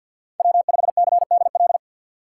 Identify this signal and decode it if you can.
Morse code, heard as a keyed tone, about 46 wpm, 700 Hz. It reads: W5CDL
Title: USA Call Signs - 5 Characters - 3x Character Spacing - 50wpm